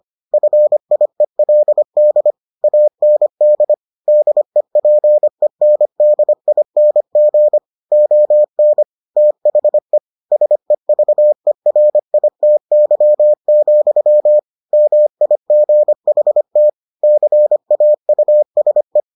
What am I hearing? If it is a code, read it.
Morse code, heard as a keyed tone, about 25 wpm, 600 Hz. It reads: FIELD AND DEPENDING ON THE SEVERITY, MIGHT CAUSE